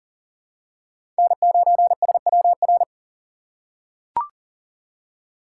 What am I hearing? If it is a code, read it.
Morse code, heard as a keyed tone, about 37 wpm, 700 Hz. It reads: N9SWR
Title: USA Call Signs - 5 Characters - 4x Character Spacing - 40wpm